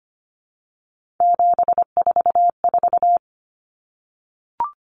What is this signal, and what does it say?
Morse code, heard as a keyed tone, about 25 wpm, 700 Hz. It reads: 744
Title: Common Cut Numbers - 3-Digit 25wpm